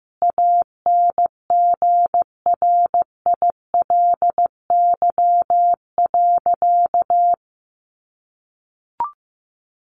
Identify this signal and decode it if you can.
Morse code, heard as a keyed tone, about 15 wpm, 700 Hz. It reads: ANGRILY.
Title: A Sound Of Thunder 15wpm